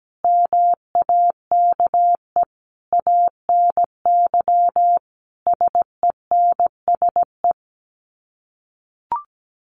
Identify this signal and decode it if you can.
Morse code, heard as a keyed tone, about 17 wpm, 700 Hz. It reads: MAKE ANY SENSE
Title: Sets of 5 Words Encore2 17wpm